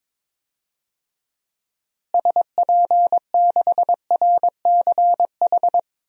Morse code, heard as a keyed tone, about 22 wpm, 700 Hz. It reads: SP6RCH
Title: DXCC-100 Call Signs - 6 Characters - Repeated 3x - 22wpm